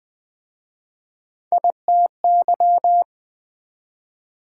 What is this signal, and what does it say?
Morse code, heard as a keyed tone, about 20 wpm, 700 Hz. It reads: ITY